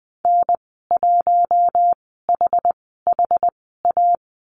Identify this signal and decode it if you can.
Morse code, heard as a keyed tone, about 20 wpm, spaced as for 14 wpm, 700 Hz. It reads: N1HHA
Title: 069 Lesson 12g - Letter d - Call Signs 20wpm